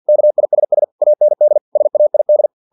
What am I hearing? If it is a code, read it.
Morse code, heard as a keyed tone, about 50 wpm, 600 Hz. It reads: KISS AND SAID
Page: Never 49wpm